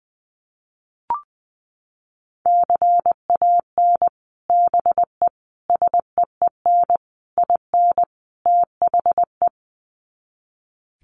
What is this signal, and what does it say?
Morse code, heard as a keyed tone, about 20 wpm, 700 Hz. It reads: CAN BE SEEN IN THE
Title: Sets of 5 Words Encore2 20wpm